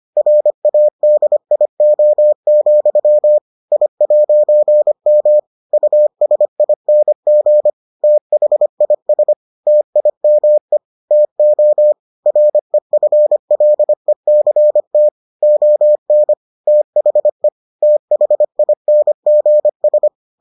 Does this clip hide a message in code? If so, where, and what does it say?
Yes: Morse code from 0.2 to 20.1 s, heard as a keyed tone, about 25 wpm, 600 Hz. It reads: RADIO, I'M USING THIS TIME TO REFLECT ON THE THINGS